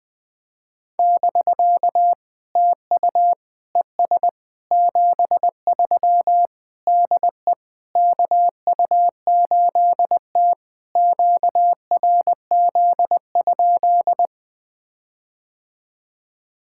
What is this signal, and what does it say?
Morse code, heard as a keyed tone, about 20 wpm, 690 Hz. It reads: <BK> TU ES 73 DE KU8T QRZ?
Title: POTA QSO Practice - 20wpm